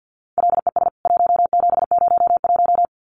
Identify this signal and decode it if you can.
Morse code, heard as a keyed tone, about 50 wpm, 700 Hz. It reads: LES 1701
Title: CWT Exchange 50wpm